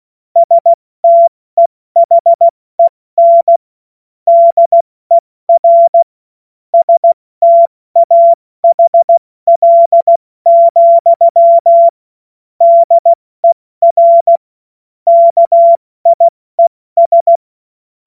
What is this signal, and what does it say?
Morse code, heard as a keyed tone, about 16 wpm, 680 Hz. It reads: STEHEN DER STAHL, DER KIES